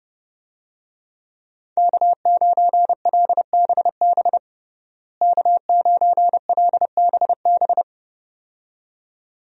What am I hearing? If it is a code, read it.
Morse code, heard as a keyed tone, about 30 wpm, 700 Hz. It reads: K9LBB K9LBB
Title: USA Call Signs - 5 Characters - Repeated 2x - 30wpm